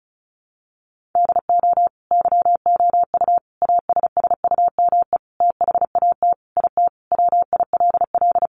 Morse code, heard as a keyed tone, about 34 wpm, 700 Hz. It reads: DO YOU ASSUME THAT IT WILL
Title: Sentences from Top 1500 Words 35wpm